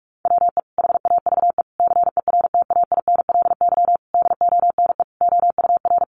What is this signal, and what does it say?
Morse code, heard as a keyed tone, about 44 wpm, 700 Hz. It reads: WE HAVE CERTAINLY DONE OUR
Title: Sentences from Top 900 Words 45wpm